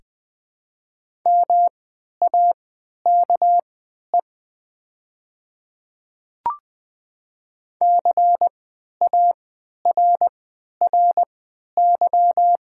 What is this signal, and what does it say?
Morse code, heard as a keyed tone, about 20 wpm, spaced as for 11 wpm, 700 Hz. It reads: MAKE CARRY
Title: 3x Character Spacing - Top 500 Words - 20wpm